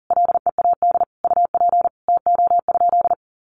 Morse code, heard as a keyed tone, about 40 wpm, 700 Hz. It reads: LEAD UP TO?
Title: Sentences from Top 500 Words 40wpm